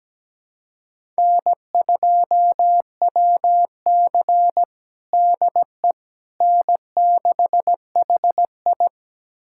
Morse code, heard as a keyed tone, about 17 wpm, 700 Hz. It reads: N2WC DE N6HI
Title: USA Callsign DE Callsign - 4 Characters - 17wpm